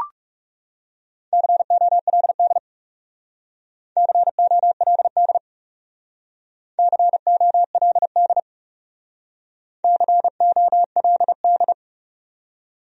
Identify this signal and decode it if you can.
Morse code, heard as a keyed tone, about 37 wpm, 700 Hz. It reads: COLD COLD COLD COLD
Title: Speed-Racing Top 500 Words 30wpm